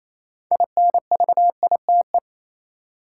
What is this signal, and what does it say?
Morse code, heard as a keyed tone, about 28 wpm, 700 Hz. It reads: INVITE
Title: Six Letter Words - 28wpm